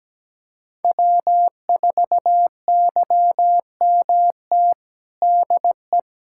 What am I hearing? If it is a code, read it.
Morse code, heard as a keyed tone, about 17 wpm, 700 Hz. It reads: W4YMT DE
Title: Practice QSO - Exchange 4 - Ending - 17wpm